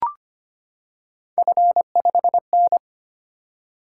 Morse code, heard as a keyed tone, about 25 wpm, 700 Hz. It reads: F5N
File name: Three Letter-Number Combinations - 25wpm